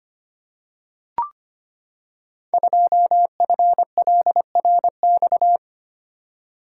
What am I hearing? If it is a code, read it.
Morse code, heard as a keyed tone, about 25 wpm, 700 Hz. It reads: 2FLRX
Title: Code Groups - 5 Letter:Numbers - 25wpm